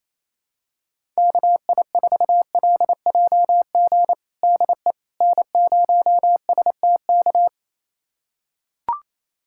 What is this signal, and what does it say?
Morse code, heard as a keyed tone, about 28 wpm, 700 Hz. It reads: KI4LJG DE N0STK
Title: USA Callsign DE Callsign 28wpm